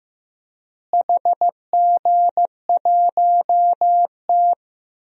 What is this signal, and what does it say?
Morse code, heard as a keyed tone, about 15 wpm, 700 Hz. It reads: HG1T
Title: DXCC-100 Call Signs - 4 Characters - Repeated 3x - 15wpm